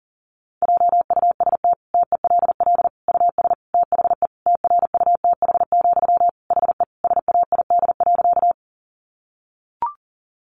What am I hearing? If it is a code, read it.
Morse code, heard as a keyed tone, about 40 wpm, 700 Hz. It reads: JUST TELL US THE TRUTH, HE SAID.
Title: Sentences from Top 800 Words 40wpm